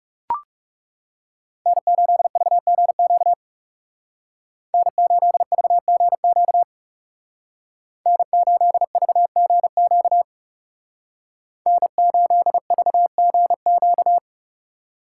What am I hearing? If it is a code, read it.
Morse code, heard as a keyed tone, about 36 wpm, 700 Hz. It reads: N8VGQ N8VGQ N8VGQ N8VGQ
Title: Speed Racing - USA Call Signs 30wpm